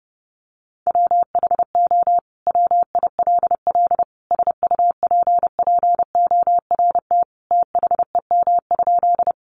Morse code, heard as a keyed tone, about 30 wpm, 700 Hz. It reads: WHO WILL SUPPORT THEM?
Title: Sentences from Top 800 Words 30wpm